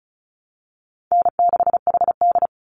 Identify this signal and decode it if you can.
Morse code, heard as a keyed tone, about 34 wpm, 700 Hz. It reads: N6HD
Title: Call Signs Repeated 2x - 35wpm